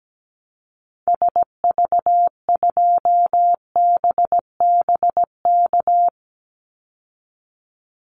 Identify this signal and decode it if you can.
Morse code, heard as a keyed tone, about 17 wpm, 700 Hz. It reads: SV2BBK
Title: Call Signs Repeated 2x - 17wpm